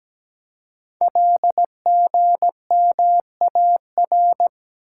Morse code, heard as a keyed tone, about 17 wpm, 700 Hz. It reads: LGMAR
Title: Code Groups - 5 Letters - 17wpm